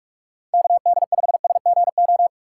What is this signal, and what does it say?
Morse code, heard as a keyed tone, about 46 wpm, 700 Hz. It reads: KD5SGO
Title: USA Call Signs - 6 Characters - 2x Character Spacing - 45wpm